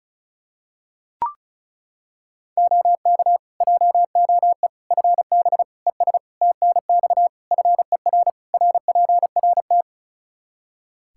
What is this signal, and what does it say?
Morse code, heard as a keyed tone, about 36 wpm, 700 Hz. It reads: OK JOE FB ES TNX FER RPRT
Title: K1USN SST QSO Practice 35wpm